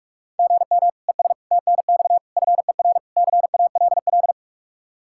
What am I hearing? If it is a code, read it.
Morse code, heard as a keyed tone, about 46 wpm, 700 Hz. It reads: GM ES TNX FER CALL
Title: Practice QSO - Exchange 1 - Beginning - 45wpm